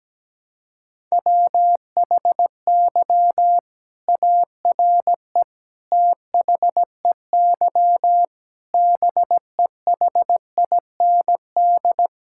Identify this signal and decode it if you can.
Morse code, heard as a keyed tone, about 17 wpm, 700 Hz. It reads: WHY ARE THEY BEHIND
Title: Sentences from Top 1000 Words - Review Part 6 - 17wpm